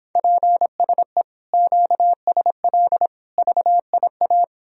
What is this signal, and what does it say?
Morse code, heard as a keyed tone, about 29 wpm, 700 Hz. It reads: PSE QSL VIA